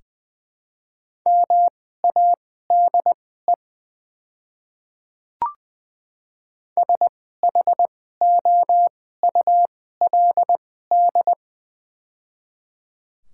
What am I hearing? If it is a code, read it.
Morse code, heard as a keyed tone, about 20 wpm, spaced as for 15 wpm, 700 Hz. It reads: MADE SHOULD
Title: 2x Character Spacing - Top 300 Words - 20wpm